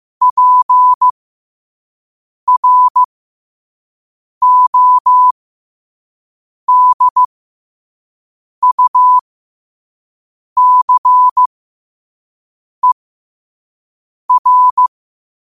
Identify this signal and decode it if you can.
Morse code, heard as a keyed tone, about 15 wpm, spaced as for 5 wpm, 1000 Hz. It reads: PRODUCER
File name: qotdfiles / 5wpm